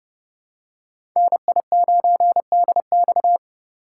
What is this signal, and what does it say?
Morse code, heard as a keyed tone, about 30 wpm, 700 Hz. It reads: NI9DX
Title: USA Call Signs - 5 Characters - 4x Character Spacing - 30wpm